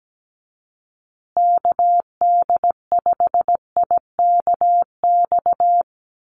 Morse code, heard as a keyed tone, about 17 wpm, 700 Hz. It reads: KD5IKX
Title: USA Call Signs Repeated 2x 17wpm